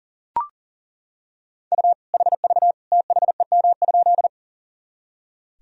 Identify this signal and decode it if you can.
Morse code, heard as a keyed tone, about 40 wpm, 700 Hz. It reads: U HV THEM?
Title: Sentences from Top 100 Words and Abbreviations 40wpm